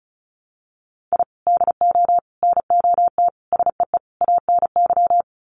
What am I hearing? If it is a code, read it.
Morse code, heard as a keyed tone, about 34 wpm, 700 Hz. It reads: I DO NOT SEE ANY
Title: Sets of 5 Words 35wpm